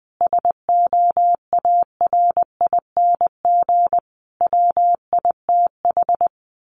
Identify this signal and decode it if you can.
Morse code, heard as a keyed tone, about 20 wpm, 700 Hz. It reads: SOARING WITH
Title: A Sound Of Thunder 20wpm